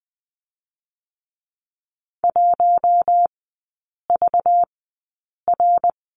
Morse code, heard as a keyed tone, about 20 wpm, spaced as for 8 wpm, 700 Hz. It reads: 1VR
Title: 218 Lesson 35e - Number 0 - Three Characters 20wpm